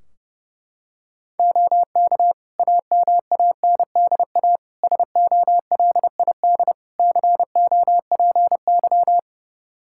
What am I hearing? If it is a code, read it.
Morse code, heard as a keyed tone, about 32 wpm, 700 Hz. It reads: OK AMANDA SOLID COPY
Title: K1USN SST QSO Practice 30wpm